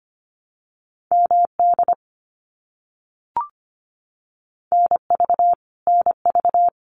Morse code, heard as a keyed tone, about 25 wpm, 700 Hz. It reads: MD NV NV
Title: US States Repeated 3x 25wpm